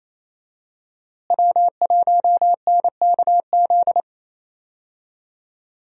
Morse code, heard as a keyed tone, about 28 wpm, 700 Hz. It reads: W1NKZ
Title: USA Call Signs - 5 Characters - Repeated 3x - 28wpm